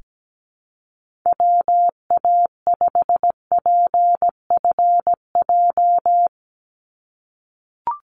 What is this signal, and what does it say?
Morse code, heard as a keyed tone, about 17 wpm, 700 Hz. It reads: WA5PFJ
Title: Call Signs - 17wpm